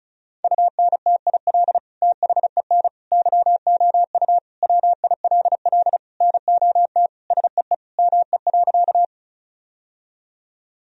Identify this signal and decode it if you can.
Morse code, heard as a keyed tone, about 36 wpm, 700 Hz. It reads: UNTIL THEN YOU WILL NOT SEE ME.
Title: Sentences from Top 1000 Words - Review Part 1 - 35wpm